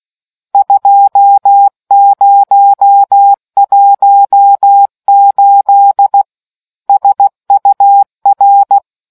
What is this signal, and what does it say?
Morse code, heard as a keyed tone, about 16 wpm, 800 Hz. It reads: 2018 SUR